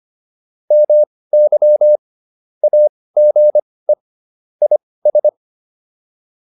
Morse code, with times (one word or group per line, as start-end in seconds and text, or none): 0.7-2.0 MY
2.6-3.9 AGE
4.6-5.3 IS